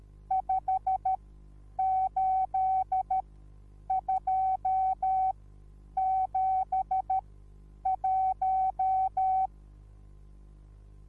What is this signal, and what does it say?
Morse code, heard as a keyed tone, about 13 wpm, spaced as for 8 wpm, 760 Hz. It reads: 58271